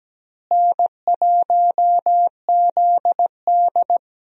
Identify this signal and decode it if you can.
Morse code, heard as a keyed tone, about 17 wpm, 700 Hz. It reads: N1ZD
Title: USA Call Signs - 4 Characters - Repeated 2x - 17wpm